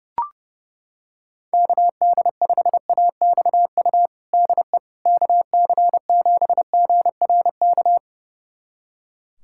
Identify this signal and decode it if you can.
Morse code, heard as a keyed tone, about 30 wpm, 700 Hz. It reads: KD5AXU DE KC7GRK
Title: USA Callsign DE Callsign - 6 Characters - 30wpm